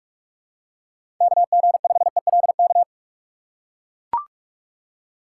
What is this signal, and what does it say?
Morse code, heard as a keyed tone, about 43 wpm, 700 Hz. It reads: KG5ELK
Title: USA Call Signs - 3x Character Spacing - 45wpm